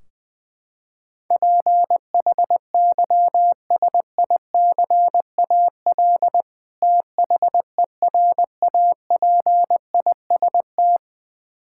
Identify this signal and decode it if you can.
Morse code, heard as a keyed tone, about 20 wpm, 700 Hz. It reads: PHYSICAL THERAPIST
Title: Practice QSO - Exchange 3 - More Info - 20wpm